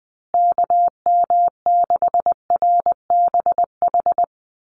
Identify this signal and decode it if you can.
Morse code, heard as a keyed tone, about 19 wpm, 700 Hz. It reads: KM6RBH
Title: USA Call Signs Repeated 3x 20wpm